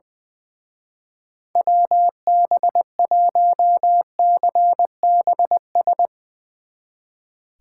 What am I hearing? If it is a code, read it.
Morse code, heard as a keyed tone, about 21 wpm, 700 Hz. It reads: WB1CBS
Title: Speed Racing - USA Call Signs 20wpm